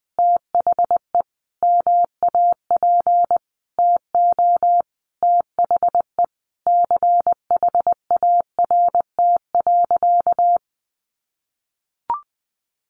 Morse code, heard as a keyed tone, about 20 wpm, 700 Hz. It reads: THE MAP TO THE CHART.
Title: Sentences from Top 1300 Words 20wpm